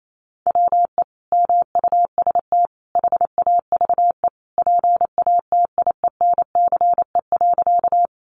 Morse code, heard as a keyed tone, about 28 wpm, 700 Hz. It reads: WE MUST HAVE PATIENCE.
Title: Sentences from Top 1300 Words 28wpm